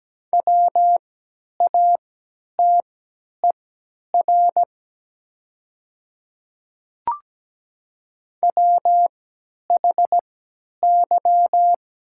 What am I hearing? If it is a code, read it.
Morse code, heard as a keyed tone, about 17 wpm, spaced as for 10 wpm, 700 Hz. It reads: WATER WHY
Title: 3x Character Spacing - Top 200 Words - 17wpm